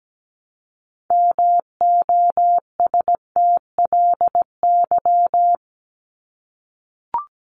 Morse code, heard as a keyed tone, about 17 wpm, 700 Hz. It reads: MOSTLY
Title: Top 1001-1250 Words 17wpm